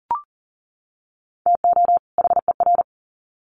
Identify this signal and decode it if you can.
Morse code, heard as a keyed tone, about 40 wpm, 700 Hz. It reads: TO HER
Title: Sets of 2 Words 40wpm